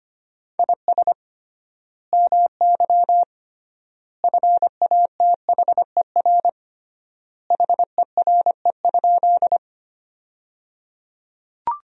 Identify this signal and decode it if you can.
Morse code, heard as a keyed tone, about 25 wpm, 700 Hz. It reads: IS MY FATHER HERE?
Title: 3x Word-Spacing - Sentences from Top 300 Words - 25wpm